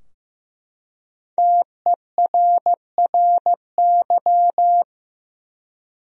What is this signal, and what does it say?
Morse code, heard as a keyed tone, about 15 wpm, 700 Hz. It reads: TERRY
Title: USA Common Names - Max 5 Letters - 15wpm